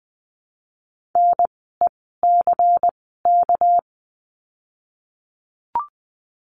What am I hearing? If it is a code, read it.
Morse code, heard as a keyed tone, about 20 wpm, spaced as for 15 wpm, 700 Hz. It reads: NECK
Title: 199 Lesson 32f - Letter k - Words 20wpm